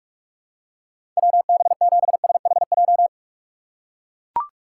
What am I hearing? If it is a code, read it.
Morse code, heard as a keyed tone, about 41 wpm, 700 Hz. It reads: WB7SHJ
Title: USA Call Signs - 2x Character Spacing - 45wpm